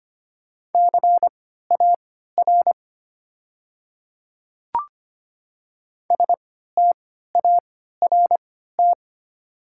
Morse code, heard as a keyed tone, about 25 wpm, spaced as for 14 wpm, 690 Hz. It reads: CAR START
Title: 3x Character Spacing - Top 500 Words - 25wpm